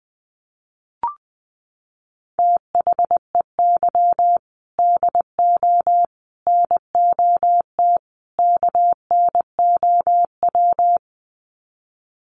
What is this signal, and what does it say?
Morse code, heard as a keyed tone, about 20 wpm, 700 Hz. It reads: THEY DO NOT KNOW
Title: Sets of 4 Words 20wpm